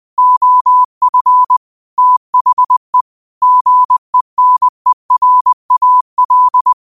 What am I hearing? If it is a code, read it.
Morse code, heard as a keyed tone, about 20 wpm, 1000 Hz. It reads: OF THE GENERAL
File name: qotdfiles / 20wpm